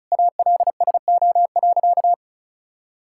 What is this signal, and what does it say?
Morse code, heard as a keyed tone, about 36 wpm, 700 Hz. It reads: ALSO.
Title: Sentences from Top 1600 Words 35wpm